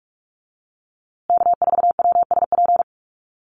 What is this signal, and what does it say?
Morse code, heard as a keyed tone, about 46 wpm, 700 Hz. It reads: K4WSP
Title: USA Call Signs Repeated 2x 45wpm